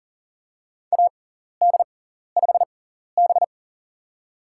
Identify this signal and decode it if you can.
Morse code, heard as a keyed tone, about 40 wpm, spaced as for 14 wpm, 700 Hz. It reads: AD5B
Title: USA Call Signs - 4 Characters - 6x Character Spacing - 40wpm